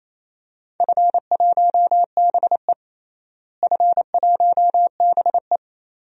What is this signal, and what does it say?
Morse code, heard as a keyed tone, about 27 wpm, 700 Hz. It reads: F1BE F1BE
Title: DXCC-100 Call Signs - 4 Characters - Repeated 2x - 28wpm